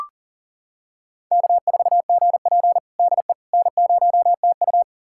Morse code, heard as a keyed tone, about 40 wpm, 700 Hz. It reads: K4GP DE N0TU
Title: USA Callsign DE Callsign - 4 Characters - 40wpm